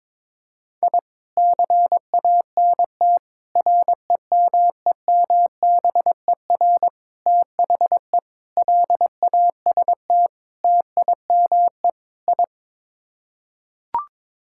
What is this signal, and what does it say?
Morse code, heard as a keyed tone, about 21 wpm, 700 Hz. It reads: I CANT REMEMBER THE LAST TIME I
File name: practice / 22wpm / Sets of 7 Words - 22wpm